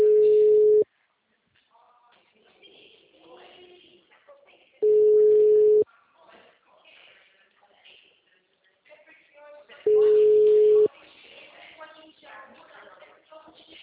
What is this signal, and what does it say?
Ringback tone (Europe, 425 Hz), with speech.